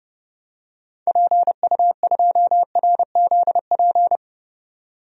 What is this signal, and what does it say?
Morse code, heard as a keyed tone, about 30 wpm, 700 Hz. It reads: PU2RZP
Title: DXCC-100 Call Signs - 6 Characters - 30wpm